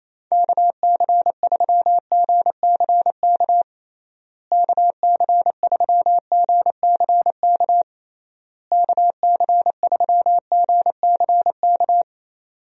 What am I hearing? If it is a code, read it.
Morse code, heard as a keyed tone, about 28 wpm, 700 Hz. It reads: KC3GCK KC3GCK KC3GCK
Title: USA Call Signs - 6 Characters - Repeated 3x - 28wpm